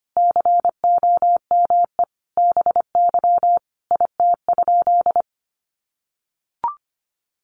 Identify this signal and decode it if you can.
Morse code, heard as a keyed tone, about 25 wpm, 700 Hz. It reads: COME BY IT?
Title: Sentences from Top 100 Words 25wpm